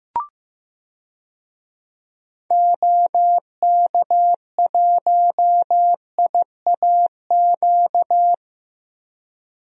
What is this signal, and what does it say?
Morse code, heard as a keyed tone, about 15 wpm, 700 Hz. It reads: OK1IAQ
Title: DXCC-100 Call Signs - 6 Characters - 15wpm